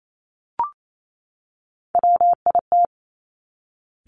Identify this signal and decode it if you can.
Morse code, heard as a keyed tone, about 28 wpm, 700 Hz. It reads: WIT
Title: Top Three Letter Combinations in Words 28wpm